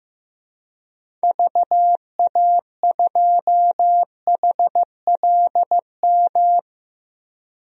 Morse code, heard as a keyed tone, about 15 wpm, 700 Hz. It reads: VA2HLM
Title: DXCC-100 Call Signs - 6 Characters - Repeated 3x - 15wpm